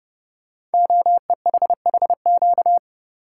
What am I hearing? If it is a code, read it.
Morse code, heard as a keyed tone, about 30 wpm, 700 Hz. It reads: OEHHQ
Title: Code Groups - 5 Letters - 30wpm